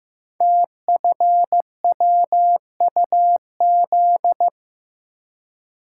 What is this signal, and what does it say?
Morse code, heard as a keyed tone, about 15 wpm, 700 Hz. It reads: TFWUZ
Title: Code Groups - 5 Letter:Numbers - 15wpm